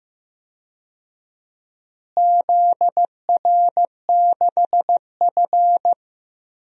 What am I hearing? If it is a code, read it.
Morse code, heard as a keyed tone, about 15 wpm, 700 Hz. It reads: ZR6F
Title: DXCC-100 Call Signs - 4 Characters - Repeated 2x - 15wpm